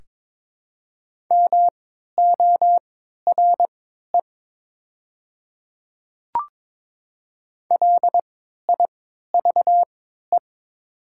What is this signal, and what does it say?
Morse code, heard as a keyed tone, about 22 wpm, spaced as for 12 wpm, 700 Hz. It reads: MORE LIVE
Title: 3x Character Spacing - Top 200 Words - 22wpm